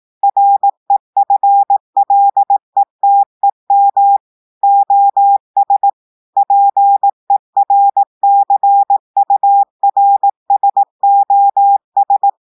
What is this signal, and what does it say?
Morse code, heard as a keyed tone, about 18 wpm, 800 Hz. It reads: REFLETEM OS PERCURSOS
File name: audios / portugais / portugais texte en clair 1 vitesse 18 mots minutes